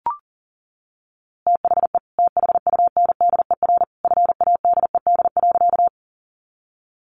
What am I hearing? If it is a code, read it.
Morse code, heard as a keyed tone, about 37 wpm, 700 Hz. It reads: THE THUNDER FADED.
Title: A Sound Of Thunder 40wpm